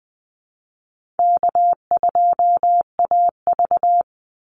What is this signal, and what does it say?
Morse code, heard as a keyed tone, about 20 wpm, 700 Hz. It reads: K2AV
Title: Easy Call Signs - 20wpm